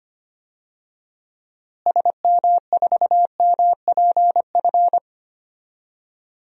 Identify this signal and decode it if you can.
Morse code, heard as a keyed tone, about 25 wpm, 700 Hz. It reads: SM4MPF
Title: DXCC-100 Call Signs - 6 Characters - Repeated 2x - 25wpm